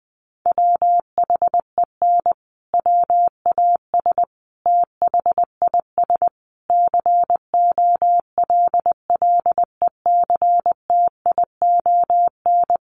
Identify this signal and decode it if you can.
Morse code, heard as a keyed tone, about 20 wpm, 700 Hz. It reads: WHEN WAS THIS COLLECTION
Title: Sentences from Top 1200 Words 20wpm